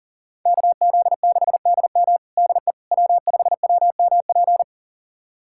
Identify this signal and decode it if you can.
Morse code, heard as a keyed tone, about 40 wpm, 700 Hz. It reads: KZ6DM DE W5WMP
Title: USA Callsign DE Callsign - 5 Characters - 40wpm